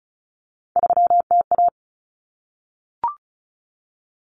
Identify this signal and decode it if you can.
Morse code, heard as a keyed tone, about 34 wpm, 700 Hz. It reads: 3TA
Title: Common Cut Numbers - 3-Digit 35wpm